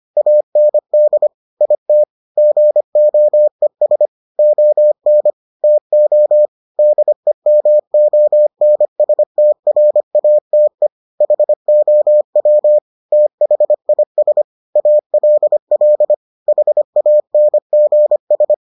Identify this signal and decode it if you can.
Morse code, heard as a keyed tone, about 25 wpm, 600 Hz. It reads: AND IT GOES ON TO DEMONSTRATE HOW THIS ALL HANGS